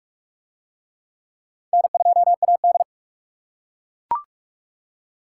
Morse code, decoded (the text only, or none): N2AD